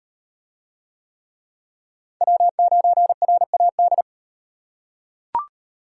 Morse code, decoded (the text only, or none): W9RAD